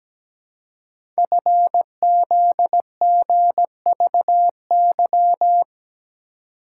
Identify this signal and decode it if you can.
Morse code, heard as a keyed tone, about 17 wpm, 700 Hz. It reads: FZGVY
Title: Code Groups - 5 Letters - 17wpm